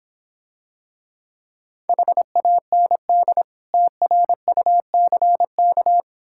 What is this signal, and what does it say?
Morse code, heard as a keyed tone, about 26 wpm, 700 Hz. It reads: HAND TRUCK